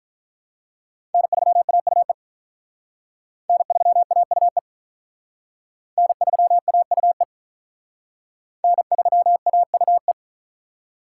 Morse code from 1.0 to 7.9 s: N3AUE N3AUE N3AUE

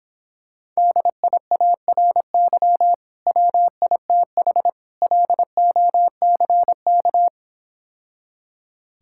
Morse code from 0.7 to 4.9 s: DIARY WITH

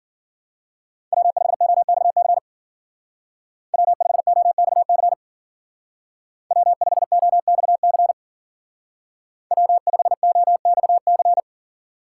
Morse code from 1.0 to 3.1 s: W5OXC

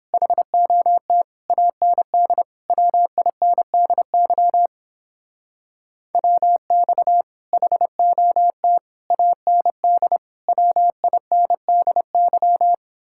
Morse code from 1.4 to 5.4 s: AND WINDY

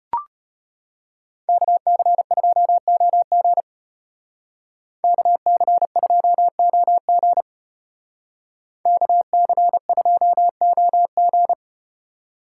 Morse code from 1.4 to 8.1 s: KC2OG KC2OG

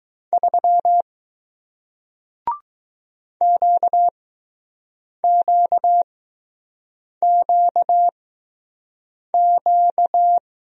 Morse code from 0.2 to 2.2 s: 3

